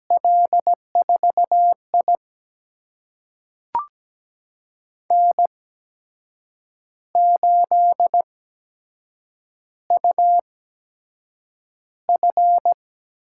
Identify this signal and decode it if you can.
Morse code, heard as a keyed tone, about 17 wpm, spaced as for 7 wpm, 700 Hz. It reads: L4I N8UF
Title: USA Call Signs - 4 Characters - 8x Character Spacing - 17wpm